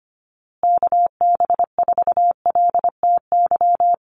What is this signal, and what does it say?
Morse code, heard as a keyed tone, about 25 wpm, 700 Hz. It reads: KB4LTY
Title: USA Call Signs Repeated 2x 25wpm